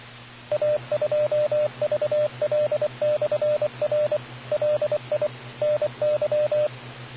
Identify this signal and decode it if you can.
Morse code, heard as a keyed tone, about 24 wpm, 600 Hz, with radio noise. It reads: A2VL/R LI NY